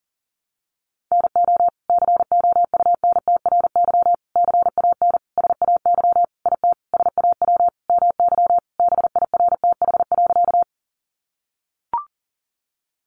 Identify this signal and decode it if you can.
Morse code, heard as a keyed tone, about 40 wpm, 690 Hz. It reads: NO COUNTRY CAN SAY IT SAW MY BIRTH.
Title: Sentences from Top 800 Words 40wpm